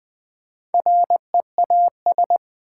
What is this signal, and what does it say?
Morse code, heard as a keyed tone, about 20 wpm, 700 Hz. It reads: REAS
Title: Build A Word - 4 Letters At A Time - 10-Letter Words - 20wpm